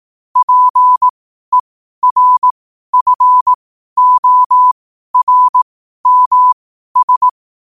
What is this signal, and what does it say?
Morse code, heard as a keyed tone, about 18 wpm, spaced as for 12 wpm, 1000 Hz. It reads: PERFORMS
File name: qotdfiles / 12wpm